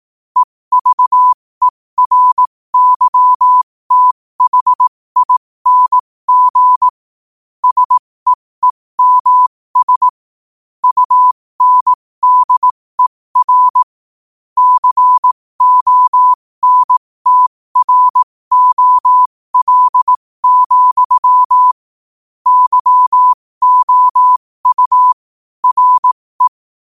Morse code, heard as a keyed tone, about 18 wpm, spaced as for 15 wpm, 1000 Hz. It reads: EVERYTHING SEEMS UNDER CONTROL, YOU RE
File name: qotdfiles / 15wpm